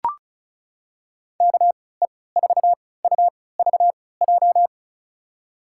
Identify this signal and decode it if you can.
Morse code, heard as a keyed tone, about 34 wpm, spaced as for 20 wpm, 700 Hz. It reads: KE4UVJ
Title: USA Call Signs - 6 Characters - 3x Character Spacing - 35wpm